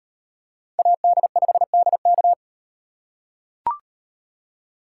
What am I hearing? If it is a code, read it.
Morse code, heard as a keyed tone, about 38 wpm, 700 Hz. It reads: AD5DK